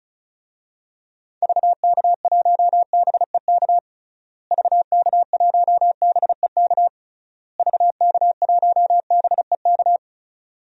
Morse code, heard as a keyed tone, about 36 wpm, 700 Hz. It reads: VK1BEK VK1BEK VK1BEK
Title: DXCC-100 Call Signs - 6 Characters - Repeated 3x - 35wpm